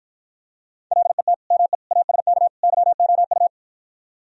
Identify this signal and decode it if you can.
Morse code, heard as a keyed tone, about 52 wpm, 700 Hz. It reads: LET ME ASK YOU